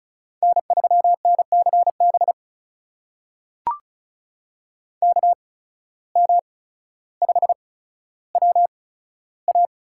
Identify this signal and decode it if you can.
Morse code, heard as a keyed tone, about 36 wpm, spaced as for 15 wpm, 700 Hz. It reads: N3NCB KM5WA
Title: USA Call Signs - 5 Characters - 8x Character Spacing - 35wpm